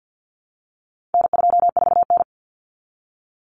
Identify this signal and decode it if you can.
Morse code, heard as a keyed tone, about 50 wpm, 700 Hz. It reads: N24N
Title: Common Cut Numbers - 4-Digit 50wpm